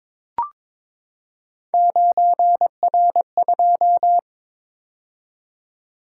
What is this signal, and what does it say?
Morse code, heard as a keyed tone, about 22 wpm, 700 Hz. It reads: 9R2
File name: practice / 22wpm / Three Letter-Number Combinations - 22wpm